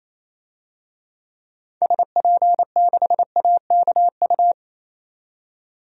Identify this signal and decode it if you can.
Morse code, heard as a keyed tone, about 28 wpm, 700 Hz. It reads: SP6AKU
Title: DXCC-100 Call Signs - 6 Characters - 28wpm